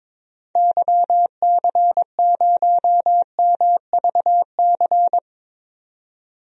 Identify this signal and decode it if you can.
Morse code, heard as a keyed tone, about 23 wpm, 700 Hz. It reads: YC0MVC
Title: DXCC-100 Call Signs - 6 Characters - 22wpm